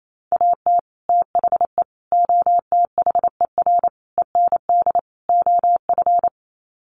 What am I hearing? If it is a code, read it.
Morse code, heard as a keyed tone, about 28 wpm, 700 Hz. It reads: AT THE OTHER END OF
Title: Sets of 5 Words 28wpm